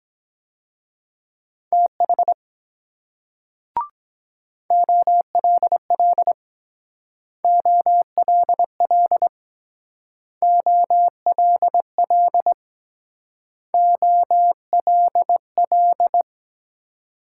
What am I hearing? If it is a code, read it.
Morse code, heard as a keyed tone, about 22 wpm, 700 Hz. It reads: TH OLL OLL OLL OLL